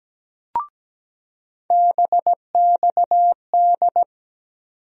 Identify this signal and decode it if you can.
Morse code, heard as a keyed tone, about 17 wpm, 700 Hz. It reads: BXD
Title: USA Call Sign Postfixes - 3 Letters 17wpm